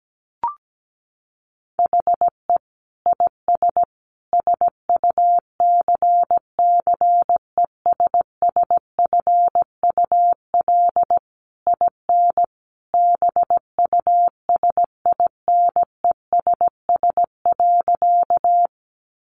Morse code, heard as a keyed tone, about 17 wpm, 700 Hz. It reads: HE IS SUCCESSFUL IN BUSINESS.
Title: Sentences from Top 900 Words 17wpm